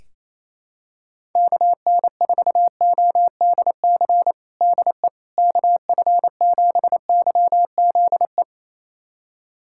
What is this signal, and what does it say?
Morse code, heard as a keyed tone, about 28 wpm, 700 Hz. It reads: KN4ODC DE KF7YZE
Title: USA Callsign DE Callsign 28wpm